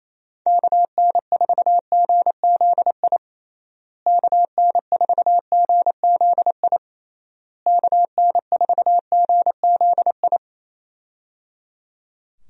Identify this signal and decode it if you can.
Morse code, heard as a keyed tone, about 28 wpm, 700 Hz. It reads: KN4GZI KN4GZI KN4GZI
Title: USA Call Signs - 6 Characters - Repeated 3x - 28wpm